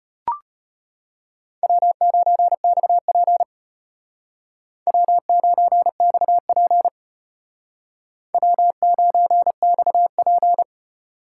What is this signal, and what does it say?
Morse code, heard as a keyed tone, about 33 wpm, 700 Hz. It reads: W9XP W9XP W9XP